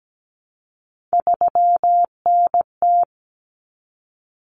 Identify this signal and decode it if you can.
Morse code, heard as a keyed tone, about 17 wpm, 700 Hz. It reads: 3NT
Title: Common Cut Numbers - 3-Digit 17wpm